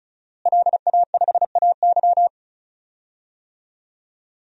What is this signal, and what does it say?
Morse code, heard as a keyed tone, about 34 wpm, 700 Hz. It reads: LA5AY
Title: DXCC-100 Call Signs - 5 Characters - Repeated 2x - 35wpm